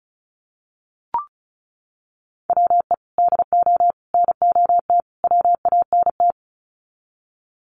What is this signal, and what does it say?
Morse code, heard as a keyed tone, about 34 wpm, 700 Hz. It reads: WE DO NOT WANT
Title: Sets of 4 Words 35wpm